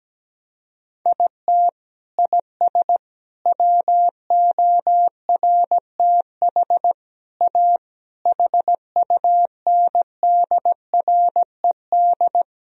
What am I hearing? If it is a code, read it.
Morse code, heard as a keyed tone, about 17 wpm, 700 Hz. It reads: IT IS WORTH A HUNDRED
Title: Sentences from Top 1000 Words - Review Part 3 - 17wpm